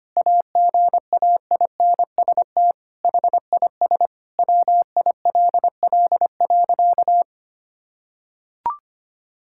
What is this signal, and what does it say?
Morse code, heard as a keyed tone, about 25 wpm, 700 Hz. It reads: AGAINST HIS WILL.
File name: practice / 25wpm / Sentences with 9-Word Phrases - 25wpm